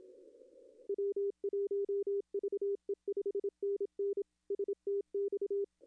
Morse code, heard as a keyed tone, about 27 wpm, 400 Hz. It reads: W1VE 5NN STX
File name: contests / arrl-160-k5tr-op-w5tn -0800